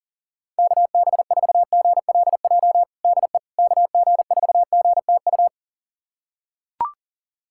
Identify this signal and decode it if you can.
Morse code, heard as a keyed tone, about 40 wpm, 700 Hz. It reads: KB4GLJ DE KG4GTU
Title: USA Callsign DE Callsign - 6 Characters - 40wpm